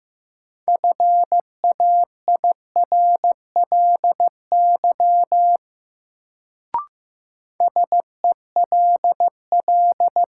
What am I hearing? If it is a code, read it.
Morse code, heard as a keyed tone, about 15 wpm, 700 Hz. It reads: FAIRLY SELL
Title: Sentences from Top 1100 Words - New Word Drill Alt - 15wpm